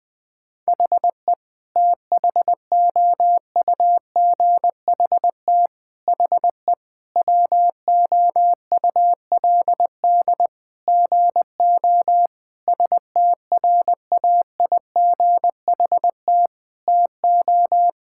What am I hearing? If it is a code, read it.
Morse code, heard as a keyed tone, about 21 wpm, 700 Hz. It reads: HE THOUGHT HE WOULD GO STRAIGHT TO
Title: Sentences from Top 1000 Words - Review Part 6 - 20wpm